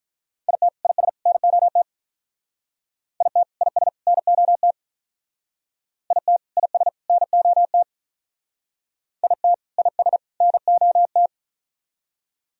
Morse code, heard as a keyed tone, about 44 wpm, 700 Hz. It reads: IT IS NOT IT IS NOT IT IS NOT IT IS NOT